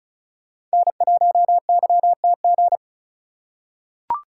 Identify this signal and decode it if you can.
Morse code, heard as a keyed tone, about 34 wpm, 700 Hz. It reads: N1YTG
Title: USA Call Signs - 5 Characters - 4x Character Spacing - 35wpm